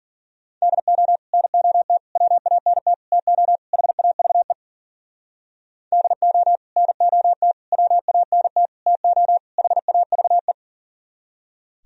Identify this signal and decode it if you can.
Morse code, heard as a keyed tone, about 52 wpm, 700 Hz. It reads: DO NOT WANT TO HAVE DO NOT WANT TO HAVE